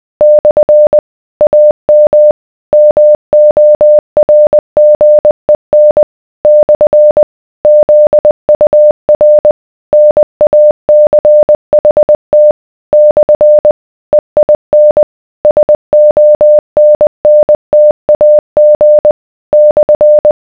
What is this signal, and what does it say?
Morse code, heard as a keyed tone, about 20 wpm, 600 Hz. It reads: / AM MORGEN / ZUR NACHT / EIN SONNTAG /